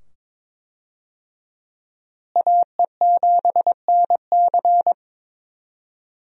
Morse code, heard as a keyed tone, about 22 wpm, 700 Hz. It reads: AE7NC
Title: USA Call Signs - 5 Characters - Repeated 2x - 22wpm